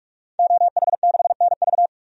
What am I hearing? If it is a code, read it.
Morse code, heard as a keyed tone, about 42 wpm, 700 Hz. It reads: OH6NV
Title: DXCC-100 Call Signs - 5 Characters - 45wpm